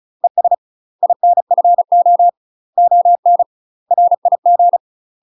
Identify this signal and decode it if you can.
Morse code, heard as a keyed tone, about 35 wpm, 700 Hz. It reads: ES INFO ON RIG
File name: UK MASTER QSOs TRAINING / 35WPM / UK MASTER QSOs 0002